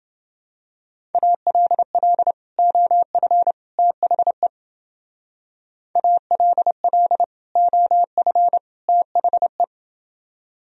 Morse code, heard as a keyed tone, about 28 wpm, 700 Hz. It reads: ALL OF THE ALL OF THE